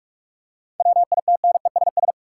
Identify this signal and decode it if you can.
Morse code, heard as a keyed tone, about 47 wpm, 700 Hz. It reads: WITNESS
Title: Seven Letter Words - 45wpm